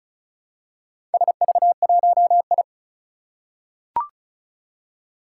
Morse code, heard as a keyed tone, about 36 wpm, 690 Hz. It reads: SV1I
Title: Four Challenging Letter-Number Combinations 35wpm